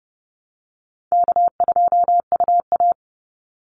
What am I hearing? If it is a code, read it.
Morse code, heard as a keyed tone, about 30 wpm, 700 Hz. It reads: K2UA
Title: Call Signs Repeated 3x - 30wpm